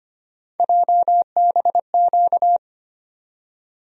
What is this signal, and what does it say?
Morse code, heard as a keyed tone, about 25 wpm, 700 Hz. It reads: JBQ
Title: USA Call Sign Postfixes - 3 Letters 25wpm